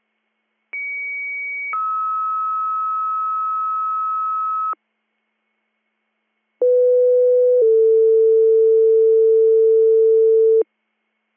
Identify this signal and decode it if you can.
Two-tone page, heard as a one-second tone, then a three-second tone, about 2250 then 1290 Hz, about 500 then 450 Hz. It is heard from 0.7 to 10.6 s.